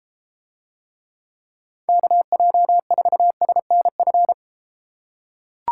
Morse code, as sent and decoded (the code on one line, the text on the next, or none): -.- .--- ....- ... -. ..-.
KJ4SNF